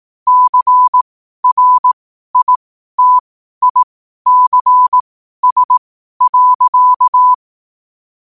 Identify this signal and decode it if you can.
Morse code, heard as a keyed tone, about 18 wpm, spaced as for 12 wpm, 1000 Hz. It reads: CRITICS.
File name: qotdfiles / 12wpm